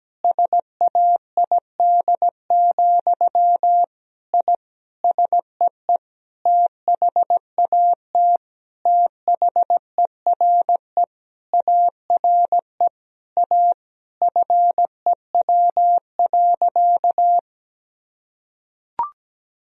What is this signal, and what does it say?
Morse code, heard as a keyed tone, about 17 wpm, 700 Hz. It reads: SAID, I SEE THAT THERE ARE A FEW.
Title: Sentences from Top 1000 Words - Review Part 1 - 17wpm